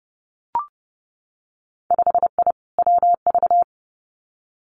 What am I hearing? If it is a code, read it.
Morse code, heard as a keyed tone, about 30 wpm, 700 Hz. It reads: 5I WV
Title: Winter Field Day Exchanges Encore 30wpm